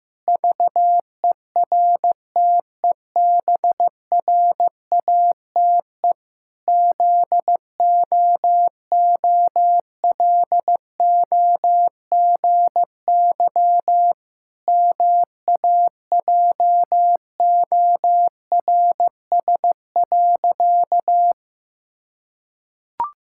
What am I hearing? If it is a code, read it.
Morse code, heard as a keyed tone, about 15 wpm, 700 Hz. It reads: VERTEBRATE ZOOLOGY MAJORS.
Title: Pangram Challenge v2 - 15wpm